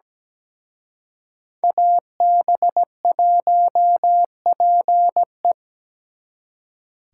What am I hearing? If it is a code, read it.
Morse code, heard as a keyed tone, about 20 wpm, 700 Hz. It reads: AB1PE
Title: Speed Racing - USA 2x2 Call Signs 17wpm